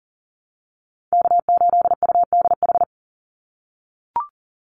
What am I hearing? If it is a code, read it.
Morse code, heard as a keyed tone, about 39 wpm, 700 Hz. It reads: K8UDH
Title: Call Signs - 40wpm